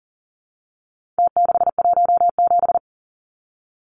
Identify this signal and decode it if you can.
Morse code, heard as a keyed tone, about 40 wpm, 700 Hz. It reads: T617
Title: Numbers - Contest Style - 4-Digit 40wpm